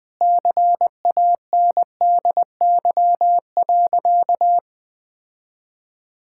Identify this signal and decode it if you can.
Morse code, heard as a keyed tone, about 21 wpm, 700 Hz. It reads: CANDY.
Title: Sentences with 4-Word Phrases - Spoken Beginning - 20wpm